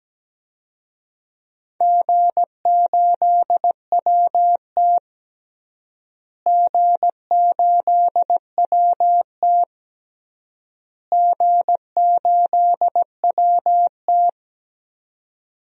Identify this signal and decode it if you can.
Morse code, heard as a keyed tone, about 17 wpm, 700 Hz. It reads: G8WT G8WT G8WT
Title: DXCC-100 Call Signs - 4 Characters - Repeated 3x - 17wpm